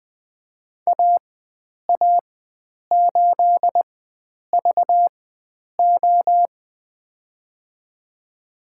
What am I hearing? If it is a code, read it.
Morse code, heard as a keyed tone, about 20 wpm, spaced as for 9 wpm, 700 Hz. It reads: AA8VO
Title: USA Call Signs - 5 Characters - 4x Character Spacing - 20wpm